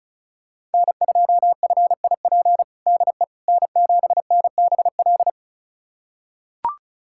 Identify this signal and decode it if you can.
Morse code, heard as a keyed tone, about 34 wpm, 690 Hz. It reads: N2FIP DE N7NBL
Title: USA Callsign DE Callsign - 5 Characters - 35wpm